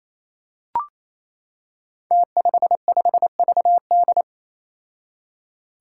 Morse code, heard as a keyed tone, about 29 wpm, 700 Hz. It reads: T55VD
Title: Five Challenging Letter-Number Combinations 28wpm